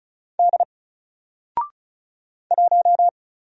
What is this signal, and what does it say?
Morse code, heard as a keyed tone, about 36 wpm, 700 Hz. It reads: D 1
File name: Single-Letter-Number-35wpm